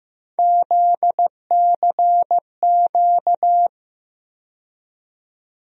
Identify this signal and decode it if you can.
Morse code, heard as a keyed tone, about 15 wpm, 700 Hz. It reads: ZCQ
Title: USA Call Sign Postfixes - 3 Letters 15wpm